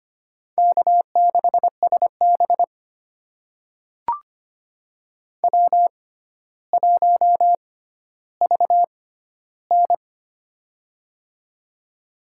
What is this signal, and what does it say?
Morse code, heard as a keyed tone, about 25 wpm, spaced as for 13 wpm, 700 Hz. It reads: K6SB W1VN
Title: USA Call Signs - 4 Characters - 6x Character Spacing - 25wpm